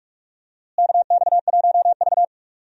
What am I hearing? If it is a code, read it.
Morse code, heard as a keyed tone, about 46 wpm, 700 Hz. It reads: KX1V
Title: USA Call Signs - 4 Characters - Repeated 3x - 45wpm